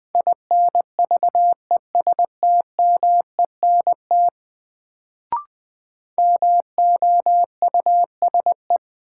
Morse code, heard as a keyed tone, about 20 wpm, 700 Hz. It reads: INVESTMENT MOUSE
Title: Sentences from Top 700 Words - New Word Drill - 20wpm